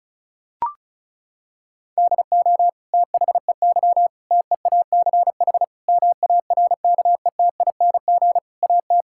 Morse code, heard as a keyed tone, about 36 wpm, 700 Hz. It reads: DO THEY TEACH MARKETING AT
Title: Sentences from Top 1000 Words - Review Part 5 - 35wpm